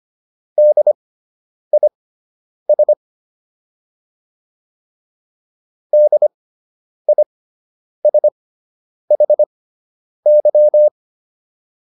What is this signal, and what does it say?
Morse code, heard as a keyed tone, about 25 wpm, spaced as for 9 wpm, 600 Hz. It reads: DIS DISHY